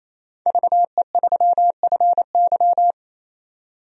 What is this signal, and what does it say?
Morse code, heard as a keyed tone, about 28 wpm, 700 Hz. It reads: VE3FY
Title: DXCC-100 Call Signs - 5 Characters - 28wpm